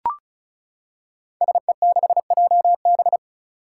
Morse code, heard as a keyed tone, about 36 wpm, 700 Hz. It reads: SE6JB
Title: Five Challenging Letter-Number Combinations 35wpm